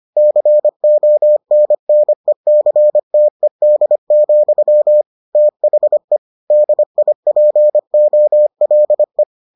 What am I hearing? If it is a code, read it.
Morse code, heard as a keyed tone, about 25 wpm, 600 Hz. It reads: CONNECTED, THE DIPOLE